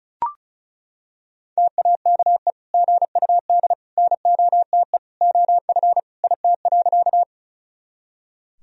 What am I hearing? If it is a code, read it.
Morse code, heard as a keyed tone, about 36 wpm, 700 Hz. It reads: TAKE GUD NOTE OF IT.
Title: Sentences from Top 500 Words and Abbreviations 35wpm